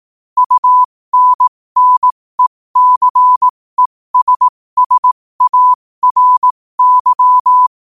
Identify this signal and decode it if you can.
Morse code, heard as a keyed tone, about 18 wpm, spaced as for 15 wpm, 1000 Hz. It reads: UNNECESSARY
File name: qotdfiles / 15wpm